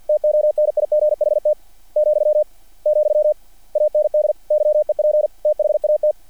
Morse code, heard as a keyed tone, about 48 wpm, 600 Hz. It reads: TONIGHT , , AND YEP THAT
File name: archived / CRSnet